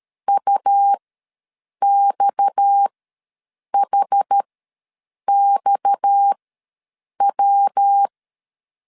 Morse code, heard as a keyed tone, about 13 wpm, spaced as for 7 wpm, 800 Hz. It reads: UXHXW
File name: LEZIONI / 6 PKX0 / 6 PKX ws 12wpm 3106